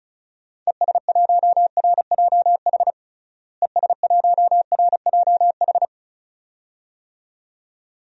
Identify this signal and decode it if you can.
Morse code, heard as a keyed tone, about 34 wpm, 700 Hz. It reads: ES1RJH ES1RJH
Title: DXCC-100 Call Signs - 6 Characters - Repeated 2x - 35wpm